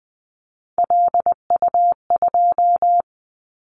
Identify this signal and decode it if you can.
Morse code, heard as a keyed tone, about 20 wpm, 700 Hz. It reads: LU2
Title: Callsign Prefixes for Top 100 DXCC - 20wpm